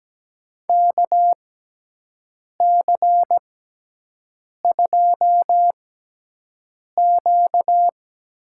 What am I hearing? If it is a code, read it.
Morse code, heard as a keyed tone, about 17 wpm, spaced as for 6 wpm, 700 Hz. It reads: KC2Q
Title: USA Call Signs - 6x Character Spacing - 17wpm